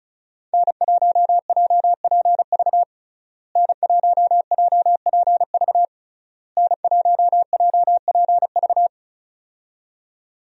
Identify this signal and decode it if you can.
Morse code, heard as a keyed tone, about 36 wpm, 700 Hz. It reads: N1JPV N1JPV N1JPV
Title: USA Call Signs - 5 Characters - Repeated 3x - 35wpm